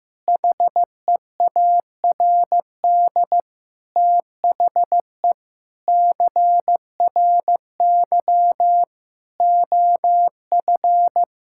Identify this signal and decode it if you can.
Morse code, heard as a keyed tone, about 15 wpm, 700 Hz. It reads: HEARD THE CRY OF
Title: Sentences from Top 1000 Words - Review Part 5 - 15wpm